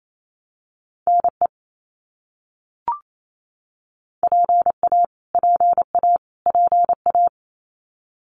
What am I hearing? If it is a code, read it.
Morse code, heard as a keyed tone, about 28 wpm, 700 Hz. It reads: NE PA PA PA
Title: US States Repeated 3x 28wpm